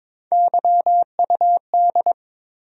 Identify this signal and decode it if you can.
Morse code, heard as a keyed tone, about 22 wpm, 700 Hz. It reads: YUD
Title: USA Call Sign Postfixes - 3 Letters 22wpm